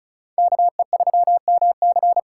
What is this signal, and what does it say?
Morse code, heard as a keyed tone, about 44 wpm, 700 Hz. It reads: KE3MC